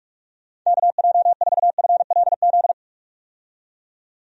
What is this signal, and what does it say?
Morse code, heard as a keyed tone, about 46 wpm, 700 Hz. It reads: KJ4FLZ
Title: USA Call Signs - 6 Characters - 45wpm